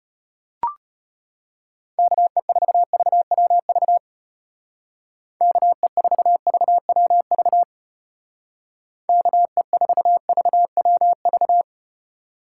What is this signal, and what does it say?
Morse code, heard as a keyed tone, about 33 wpm, 700 Hz. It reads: KE4VWV KE4VWV KE4VWV